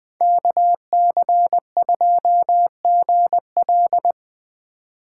Morse code, heard as a keyed tone, about 20 wpm, 700 Hz. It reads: KC2GL
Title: Speed Racing - USA Call Signs 17wpm